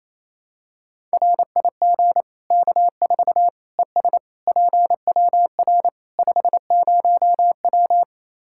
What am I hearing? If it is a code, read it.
Morse code, heard as a keyed tone, about 28 wpm, 700 Hz. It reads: RIG K4 ES PWR 50W
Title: Practice QSO - Exchange 2 - Info - 28wpm